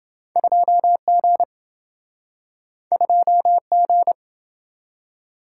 Morse code, heard as a keyed tone, about 27 wpm, 700 Hz. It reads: 2G 2G